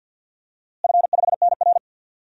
Morse code, heard as a keyed tone, about 52 wpm, 700 Hz. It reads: F5NR
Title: DXCC-100 Call Signs - 4 Characters - Repeated 3x - 50wpm